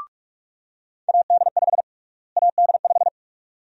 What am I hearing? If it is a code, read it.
Morse code, heard as a keyed tone, about 42 wpm, 700 Hz. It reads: AD5 AD5
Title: USA Call Sign Prefixes - Repeated 2x - 45wpm